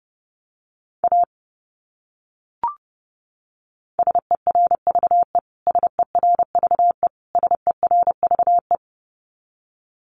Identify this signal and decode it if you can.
Morse code, heard as a keyed tone, about 30 wpm, 700 Hz. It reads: A SERVE SERVE SERVE
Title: Top 500 Words Repeated 30wpm